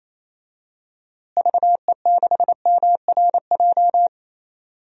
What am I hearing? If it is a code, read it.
Morse code, heard as a keyed tone, about 28 wpm, 700 Hz. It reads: VE6MRJ
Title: DXCC-100 Call Signs - 6 Characters - Repeated 2x - 28wpm